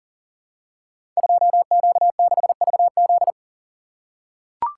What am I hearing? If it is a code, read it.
Morse code, heard as a keyed tone, about 40 wpm, 700 Hz. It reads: 2Q6VZ
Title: Code Groups - 5 Letter:Numbers - 40wpm